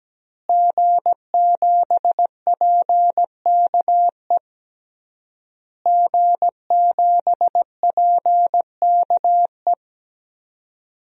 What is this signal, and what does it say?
Morse code, heard as a keyed tone, about 17 wpm, 700 Hz. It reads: G7PKE G7PKE
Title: DXCC-100 Call Signs - 5 Characters - Repeated 3x - 17wpm